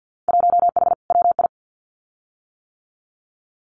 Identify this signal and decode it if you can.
Morse code, heard as a keyed tone, about 50 wpm, 700 Hz. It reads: 1H WI
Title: Winter Field Day Exchanges 50wpm